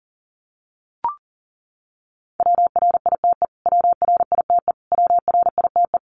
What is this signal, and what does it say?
Morse code, heard as a keyed tone, about 40 wpm, 700 Hz. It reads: WRITE WRITE WRITE
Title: Top 500 Words Repeated 40wpm